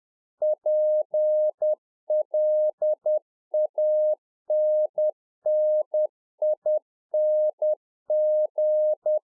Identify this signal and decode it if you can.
Morse code, heard as a keyed tone, about 10 wpm, 610 Hz. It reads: PLANNING
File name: 2011-04-mp3 / RICK JJ0MPI 10WPM